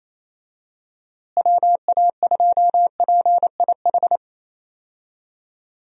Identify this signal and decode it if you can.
Morse code, heard as a keyed tone, about 28 wpm, 700 Hz. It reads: WA2PIH
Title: USA Call Signs - 6 Characters - Repeated 3x - 28wpm